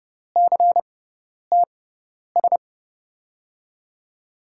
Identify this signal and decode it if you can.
Morse code, heard as a keyed tone, about 31 wpm, spaced as for 10 wpm, 700 Hz. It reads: CTS
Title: USA Call Signs - 5 Characters - 6x Character Spacing - 30wpm